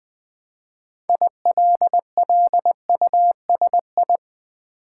Morse code, heard as a keyed tone, about 20 wpm, 700 Hz. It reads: ILLUSI
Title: Build A Word - 3 Letters At A Time - 8-Letter Words - 20wpm